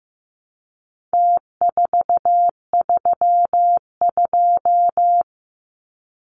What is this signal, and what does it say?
Morse code, heard as a keyed tone, about 15 wpm, 700 Hz. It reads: T432
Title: Numbers - Contest Style - 4-Digit 15wpm